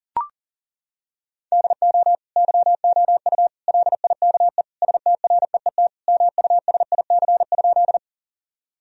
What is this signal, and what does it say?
Morse code, heard as a keyed tone, about 40 wpm, 700 Hz. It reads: DO YOU LIKE STREET MUSIC?
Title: Sentences from Top 1000 Words - Review Part 3 - 40wpm